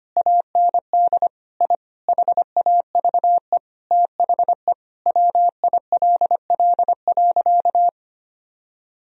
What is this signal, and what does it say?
Morse code, heard as a keyed tone, about 25 wpm, 700 Hz. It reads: AND I HAVE THE WILL.
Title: Sentences from Top 1000 Words - Review Part 2 - 25wpm